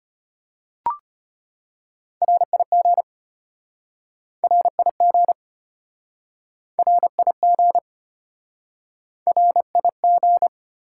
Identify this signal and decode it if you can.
Morse code, heard as a keyed tone, about 33 wpm, 700 Hz. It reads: RIG RIG RIG RIG